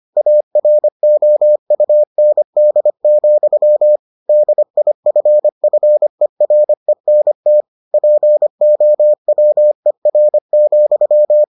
Morse code, heard as a keyed tone, about 25 wpm, 600 Hz. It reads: AROUND, DIFFERENT POWER,